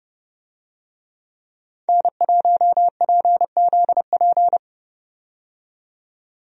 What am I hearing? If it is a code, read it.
Morse code, heard as a keyed tone, about 30 wpm, 700 Hz. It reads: N1PZP
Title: USA Call Signs - 5 Characters - 30wpm